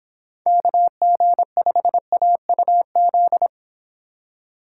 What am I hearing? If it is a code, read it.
Morse code, heard as a keyed tone, about 28 wpm, 700 Hz. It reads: KG5AUZ
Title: Speed Racing - USA 2x3 Call Signs 22wpm